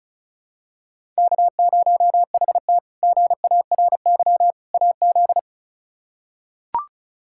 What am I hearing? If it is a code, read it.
Morse code, heard as a keyed tone, about 34 wpm, 700 Hz. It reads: K0HT GARY AZ
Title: K1USN SST QSO Practice - 35wpm